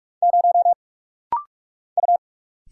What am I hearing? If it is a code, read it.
Morse code, heard as a keyed tone, about 44 wpm, 690 Hz. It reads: 0 U
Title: Single Letter-Number - Rapid-Fire 45wpm